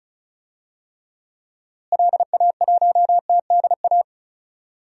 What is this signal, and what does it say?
Morse code, heard as a keyed tone, about 36 wpm, 700 Hz. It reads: LA1TDA
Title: DXCC-100 Call Signs - 6 Characters - Repeated 2x - 35wpm